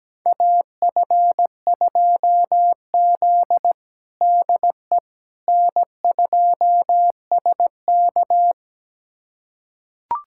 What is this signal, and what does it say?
Morse code, heard as a keyed tone, about 17 wpm, 700 Hz. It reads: AF2Z DE N2SK
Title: USA Callsign DE Callsign - 4 Characters - 17wpm